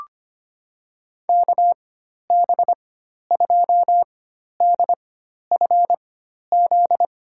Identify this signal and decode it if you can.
Morse code, heard as a keyed tone, about 25 wpm, spaced as for 12 wpm, 700 Hz. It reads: KB2DFZ
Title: USA Call Signs - 6 Characters - 4x Character Spacing - 25wpm